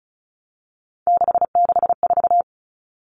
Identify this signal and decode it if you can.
Morse code, heard as a keyed tone, about 34 wpm, 700 Hz. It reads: 664
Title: Common Cut Numbers - 3-Digit 35wpm